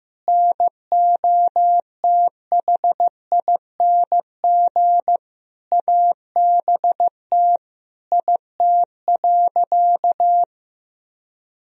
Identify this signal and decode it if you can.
Morse code, heard as a keyed tone, about 15 wpm, 700 Hz. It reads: NOTHING ABT IT.
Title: Sentences from Top 500 Words and Abbreviations 15wpm